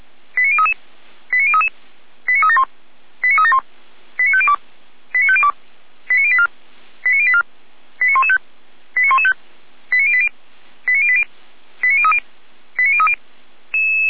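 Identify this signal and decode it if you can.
Five-tone selective call (ZVEI), heard as quick runs of five tones, tones of about 80 ms. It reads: 89033 89033 89371 89371 89502 89502 89084 89084 89115 89115 89080 89080 89030 89030
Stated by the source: ZVEI five-tone selective call 89120, received by SDR.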